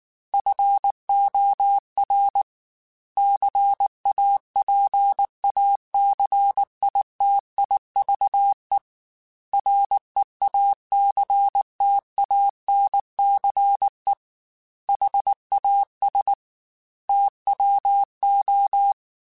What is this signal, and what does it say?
Morse code, heard as a keyed tone, about 19 wpm, 800 Hz. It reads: FOR CAPACITIVE REACTANCE HAS TWO
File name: Technical Subjects / Technical Subjects-mp3s / XC @18